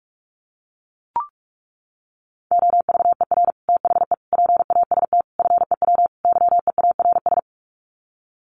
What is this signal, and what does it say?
Morse code, heard as a keyed tone, about 44 wpm, 700 Hz. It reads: OVER THE PAST FEW YEARS
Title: Sets of 5 Words Encore 45wpm